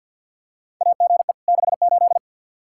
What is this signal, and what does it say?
Morse code, heard as a keyed tone, about 52 wpm, 700 Hz. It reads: AGE 68
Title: Practice QSO - Exchange 3 - More Info - 50wpm